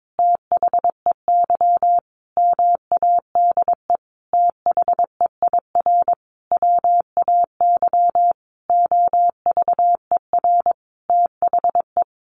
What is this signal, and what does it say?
Morse code, heard as a keyed tone, about 22 wpm, 700 Hz. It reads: THEY MADE THEIR WAY OVER THE
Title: Sentences from Top 1500 Words 22wpm